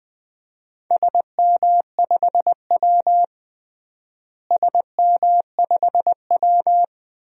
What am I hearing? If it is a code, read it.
Morse code, heard as a keyed tone, about 21 wpm, 700 Hz. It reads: SM5W SM5W
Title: DXCC-100 Call Signs - 4 Characters - Repeated 2x - 20wpm